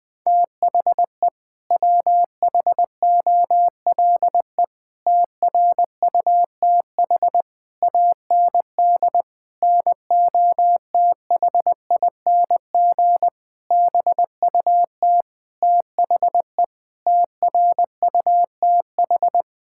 Morse code, heard as a keyed tone, about 20 wpm, 700 Hz. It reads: THE WHOLE TRUTH AND NOTHING BUT THE TRUTH
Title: Sets of 8 Words - 20wpm